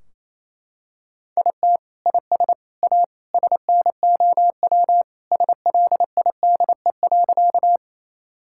Morse code, heard as a keyed tone, about 28 wpm, 700 Hz. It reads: IT IS A SNOW SLIDE.
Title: Sentences from Top 1600 Words 28wpm